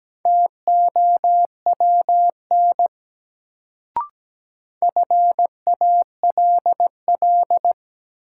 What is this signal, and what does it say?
Morse code, heard as a keyed tone, about 17 wpm, 700 Hz. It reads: TOWN FALL
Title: Sentences from Top 500 Words - New Word Drill - 17wpm